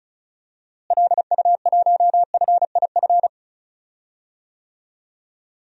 Morse code, heard as a keyed tone, about 36 wpm, 700 Hz. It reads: LU1FIF
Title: DXCC-100 Call Signs - 6 Characters - Repeated 2x - 35wpm